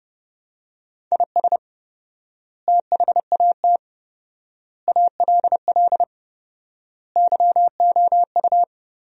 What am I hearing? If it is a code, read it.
Morse code, heard as a keyed tone, about 30 wpm, 700 Hz. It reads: IS THAT ALL YOU
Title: 4x Word-Spacing - Sentences from Top 200 Words - 30wpm